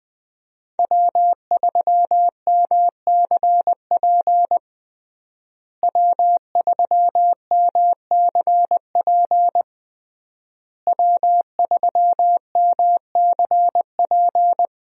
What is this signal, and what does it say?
Morse code, heard as a keyed tone, about 21 wpm, 700 Hz. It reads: W3MCP W3MCP W3MCP
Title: USA Call Signs - 5 Characters - Repeated 3x - 20wpm